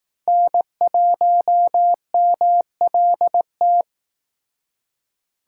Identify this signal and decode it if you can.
Morse code, heard as a keyed tone, about 19 wpm, 700 Hz. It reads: N1MLT
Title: Speed Racing - USA 1x3 Call Signs 15wpm